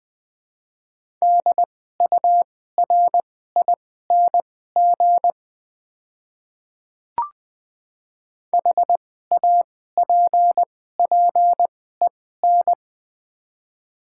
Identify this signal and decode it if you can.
Morse code, heard as a keyed tone, about 20 wpm, spaced as for 15 wpm, 700 Hz. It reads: DURING HAPPEN
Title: 2x Character Spacing - Top 500 Words - 20wpm